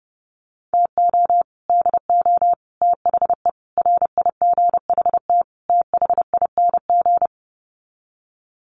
Morse code, heard as a keyed tone, about 30 wpm, 700 Hz. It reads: TO DO THE RIGHT THING
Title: Sets of 5 Words 30wpm